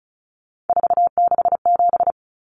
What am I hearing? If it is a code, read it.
Morse code, heard as a keyed tone, about 34 wpm, 700 Hz. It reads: 467
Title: Numbers - 3-Digit 35wpm